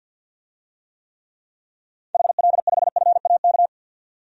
Morse code, heard as a keyed tone, about 46 wpm, 700 Hz. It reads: HL5FAK
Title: DXCC-100 Call Signs - 6 Characters - 50wpm